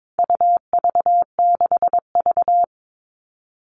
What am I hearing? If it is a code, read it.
Morse code, heard as a keyed tone, about 22 wpm, 700 Hz. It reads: UV6V
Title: All Cut Numbers - 4-Digit 22wpm